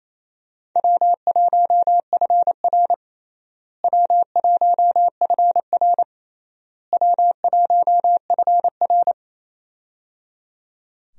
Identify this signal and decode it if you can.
Morse code, heard as a keyed tone, about 28 wpm, 700 Hz. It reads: W1FR W1FR W1FR
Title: USA Call Signs - 4 Characters - Repeated 3x - 28wpm